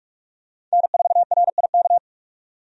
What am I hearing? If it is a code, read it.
Morse code, heard as a keyed tone, about 46 wpm, 700 Hz. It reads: N4RIK
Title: USA Call Signs - 5 Characters - 45wpm